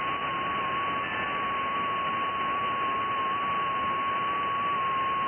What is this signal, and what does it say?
FT8.
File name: IC-7100